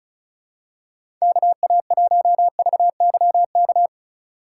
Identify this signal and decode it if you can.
Morse code, heard as a keyed tone, about 36 wpm, 700 Hz. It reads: KA1VYK
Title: USA Call Signs - 6 Characters - 2x Character Spacing - 35wpm